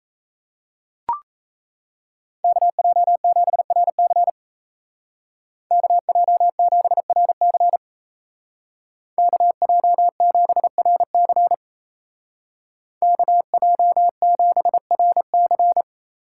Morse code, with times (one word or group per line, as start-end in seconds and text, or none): 2.4-4.3 KJ7RC
5.7-7.8 KJ7RC
9.2-11.5 KJ7RC
13.0-15.8 KJ7RC